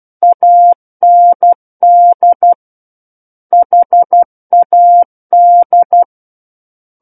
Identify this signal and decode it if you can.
Morse code, heard as a keyed tone, about 12 wpm, 700 Hz. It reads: AND HAD
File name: Book One / 03-Chapter 3 - 12 WPM